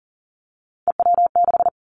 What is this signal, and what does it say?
Morse code, heard as a keyed tone, about 40 wpm, 700 Hz. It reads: EW6
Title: Callsign Prefixes for Top 100 DXCC - 40wpm